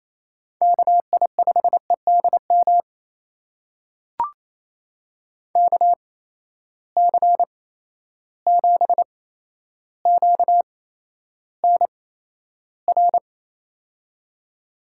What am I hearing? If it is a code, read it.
Morse code, heard as a keyed tone, about 28 wpm, spaced as for 12 wpm, 700 Hz. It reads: KI5EDM KC7QNR
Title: USA Call Signs - 6 Characters - 8x Character Spacing - 28wpm